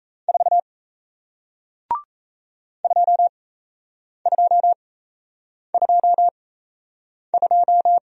Morse code, heard as a keyed tone, about 37 wpm, spaced as for 8 wpm, 700 Hz. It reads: 4 2222